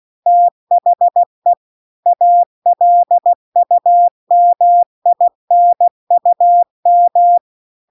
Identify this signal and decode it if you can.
Morse code, heard as a keyed tone, about 16 wpm, 700 Hz. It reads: THE ALUMINUM